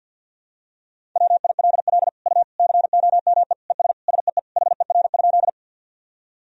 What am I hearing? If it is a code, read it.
Morse code, heard as a keyed tone, about 48 wpm, 700 Hz. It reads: WILL U COME ES SEE HER?
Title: Sentences from Top 100 Words and Abbreviations 50wpm